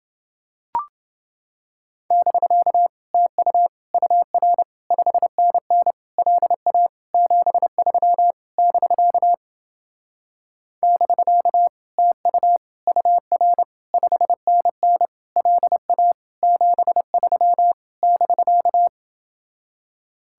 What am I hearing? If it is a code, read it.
Morse code, heard as a keyed tone, about 28 wpm, 700 Hz. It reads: <BK> TU UR 5NN LA 73 <BK> <BK> TU UR 5NN LA 73 <BK>